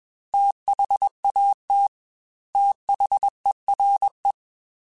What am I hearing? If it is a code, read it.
Morse code, heard as a keyed tone, about 21 wpm, 800 Hz. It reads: THAT THERE
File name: Technical Subjects / Technical Subjects-mp3s / AND Gate @20